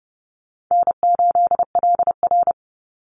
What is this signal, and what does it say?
Morse code, heard as a keyed tone, about 30 wpm, 700 Hz. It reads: N8LR
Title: Easy Call Signs - 30wpm